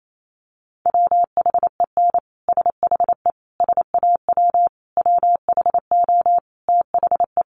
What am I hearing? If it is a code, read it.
Morse code, heard as a keyed tone, about 28 wpm, 700 Hz. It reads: WHEN SHE SAW WHO THE
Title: Sentences from Top 1200 Words 28wpm